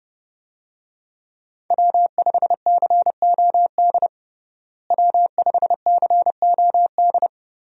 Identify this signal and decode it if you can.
Morse code, heard as a keyed tone, about 30 wpm, 700 Hz. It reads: W5COD W5COD
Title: USA Call Signs - 5 Characters - Repeated 3x - 30wpm